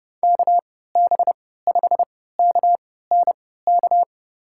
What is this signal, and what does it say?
Morse code, heard as a keyed tone, about 31 wpm, spaced as for 17 wpm, 700 Hz. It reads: KB5KNK
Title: USA Call Signs - 3x Character Spacing - 30wpm